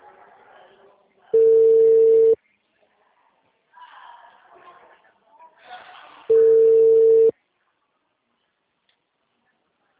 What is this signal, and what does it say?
Ringback tone (China, 450 Hz), with speech.